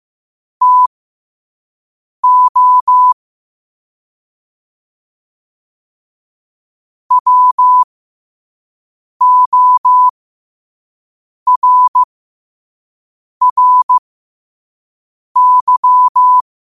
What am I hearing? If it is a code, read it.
Morse code, heard as a keyed tone, about 15 wpm, spaced as for 5 wpm, 1000 Hz. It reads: TO WORRY
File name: qotdfiles / 5wpm